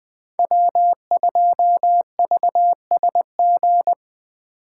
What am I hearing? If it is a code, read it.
Morse code, heard as a keyed tone, about 22 wpm, 700 Hz. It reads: W2VSG